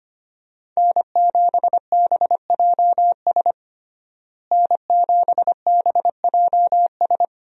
Morse code, heard as a keyed tone, about 25 wpm, 700 Hz. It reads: N7BJS N7BJS
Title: USA Call Signs - 5 Characters - Repeated 2x - 25wpm